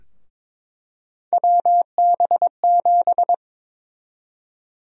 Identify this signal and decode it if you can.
Morse code, heard as a keyed tone, about 22 wpm, 700 Hz. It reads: WB7
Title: USA Call Sign Prefixes - Repeated 2x - 22wpm